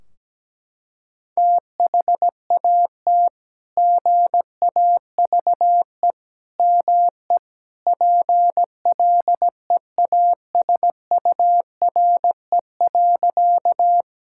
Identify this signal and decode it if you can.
Morse code, heard as a keyed tone, about 17 wpm, 700 Hz. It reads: THAT GAVE ME PLEASURE.
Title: Sentences from Top 1600 Words 17wpm